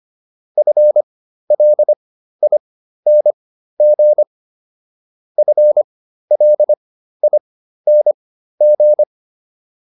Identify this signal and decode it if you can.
Morse code, heard as a keyed tone, about 25 wpm, spaced as for 13 wpm, 600 Hz. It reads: FLING FLING